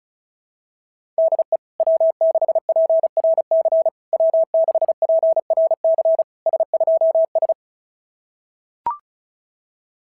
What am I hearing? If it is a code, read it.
Morse code, heard as a keyed tone, about 36 wpm, 660 Hz. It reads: DE W6PRC W6PRC S2S
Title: SOTA QSO Practice 35wpm